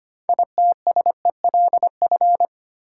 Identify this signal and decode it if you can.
Morse code, heard as a keyed tone, about 25 wpm, 700 Hz. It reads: ITSELF
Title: Sentences from Top 800 Words - New Word Drill - 25wpm